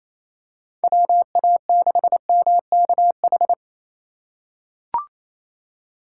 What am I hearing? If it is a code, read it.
Morse code, heard as a keyed tone, about 27 wpm, 700 Hz. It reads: WA6MKH
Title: USA Call Signs - 4x Character Spacing - 28wpm